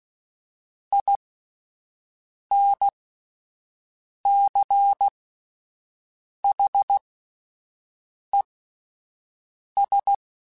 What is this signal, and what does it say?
Morse code, heard as a keyed tone, about 16 wpm, spaced as for 5 wpm, 800 Hz. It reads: INCHES
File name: (spaced) / QSOs Spaced-mp3s / WA7QOX 1 @10 sp